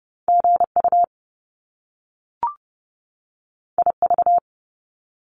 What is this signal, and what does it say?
Morse code, heard as a keyed tone, about 30 wpm, 700 Hz. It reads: GU IV
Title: Top Two Letter Combinations in Words 30wpm